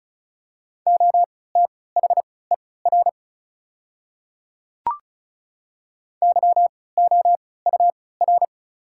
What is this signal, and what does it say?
Morse code, heard as a keyed tone, about 34 wpm, spaced as for 20 wpm, 700 Hz. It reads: OTHER YOUR
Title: 3x Character Spacing - Top 200 Words - 35wpm